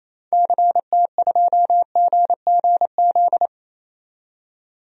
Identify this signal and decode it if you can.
Morse code, heard as a keyed tone, about 28 wpm, 700 Hz. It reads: CT2GGZ
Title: DXCC-100 Call Signs - 6 Characters - 28wpm